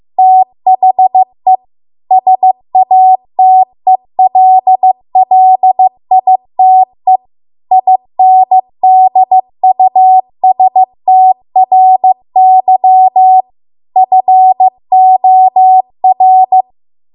Morse code, heard as a keyed tone, about 15 wpm, 750 Hz. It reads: THE SATELLITE INDUSTRY FOR
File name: Archive / 15 WPM / 15WPM